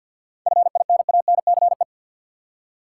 Morse code, heard as a keyed tone, about 51 wpm, 700 Hz. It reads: FINANCE
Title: Seven Letter Words - 50wpm